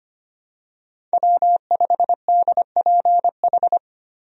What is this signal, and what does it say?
Morse code, heard as a keyed tone, about 25 wpm, 700 Hz. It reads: W5DPH
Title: USA Call Signs - 5 Characters - Repeated 2x - 25wpm